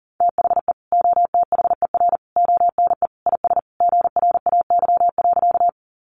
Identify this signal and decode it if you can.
Morse code, heard as a keyed tone, about 40 wpm, 700 Hz. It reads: THE OTHER ONE IS GRAY.
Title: Additional Easy Phrases 40wpm